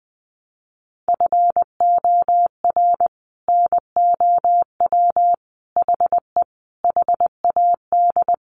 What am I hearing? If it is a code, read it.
Morse code, heard as a keyed tone, about 20 wpm, 700 Hz. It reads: FOR NOW HE HAD
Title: Sentences from Top 1100 Words 20wpm